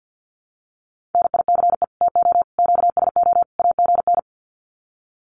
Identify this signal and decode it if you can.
Morse code, heard as a keyed tone, about 50 wpm, 700 Hz. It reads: NICE TO QSO AGN
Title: Common QSO Phrases 50wpm